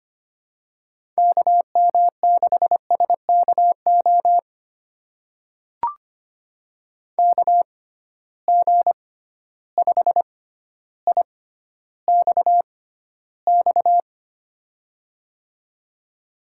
Morse code, heard as a keyed tone, about 25 wpm, spaced as for 13 wpm, 700 Hz. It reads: KM6SKO KG5IXX
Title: USA Call Signs - 6 Characters - 6x Character Spacing - 25wpm